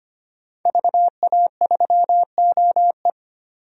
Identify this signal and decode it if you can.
Morse code, heard as a keyed tone, about 25 wpm, 700 Hz. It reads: VA3OE
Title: DXCC-100 Call Signs - 5 Characters - 25wpm